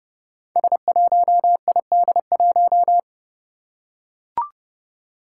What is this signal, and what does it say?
Morse code, heard as a keyed tone, about 30 wpm, 700 Hz. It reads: S1ID1
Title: Five Challenging Letter-Number Combinations 30wpm